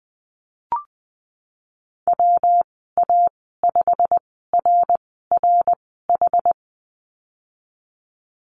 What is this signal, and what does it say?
Morse code, heard as a keyed tone, about 19 wpm, spaced as for 14 wpm, 700 Hz. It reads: WA5RRH
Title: 140 Lesson 23g - Letter f - Call Signs 20wpm